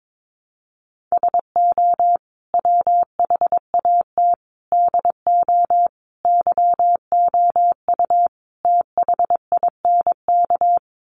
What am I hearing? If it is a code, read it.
Morse code, heard as a keyed tone, about 22 wpm, 700 Hz. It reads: SO WHAT DO YOU THINK
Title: Sets of 5 Words Encore2 22wpm